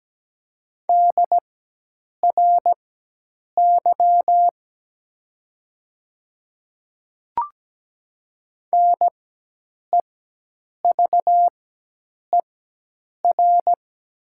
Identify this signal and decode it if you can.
Morse code, heard as a keyed tone, about 17 wpm, spaced as for 8 wpm, 700 Hz. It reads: DRY NEVER
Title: 4x Character Spacing - Top 500 Words - 17wpm